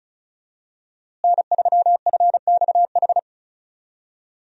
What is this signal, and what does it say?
Morse code, heard as a keyed tone, about 34 wpm, 700 Hz. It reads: N3FXH
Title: USA Call Signs - 5 Characters - Repeated 2x - 35wpm